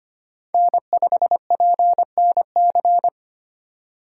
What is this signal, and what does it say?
Morse code, heard as a keyed tone, about 25 wpm, 700 Hz. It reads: N5PNC
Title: USA Call Signs - 3x Character Spacing - 25wpm